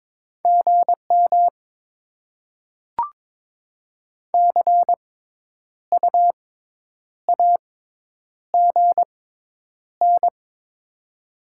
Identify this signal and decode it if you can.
Morse code, heard as a keyed tone, about 22 wpm, spaced as for 9 wpm, 700 Hz. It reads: GM CU AGN
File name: Top 20 QSO Elements - 6x Character Spacing - 22wpm